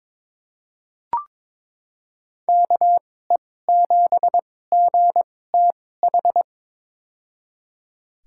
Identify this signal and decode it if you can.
Morse code, heard as a keyed tone, about 22 wpm, spaced as for 16 wpm, 700 Hz. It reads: KE7GTH
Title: USA Call Signs - 6 Characters - 2x Character Spacing - 22wpm